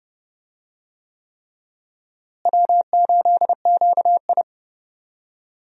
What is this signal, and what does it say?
Morse code, heard as a keyed tone, about 30 wpm, 700 Hz. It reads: W8QI
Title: USA Call Signs - 4 Characters - Repeated 3x - 30wpm